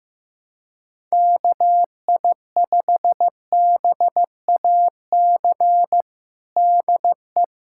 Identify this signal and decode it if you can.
Morse code, heard as a keyed tone, about 15 wpm, 700 Hz. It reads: KI5BAC DE
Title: USA Callsign DE Callsign - 6 Characters - 15wpm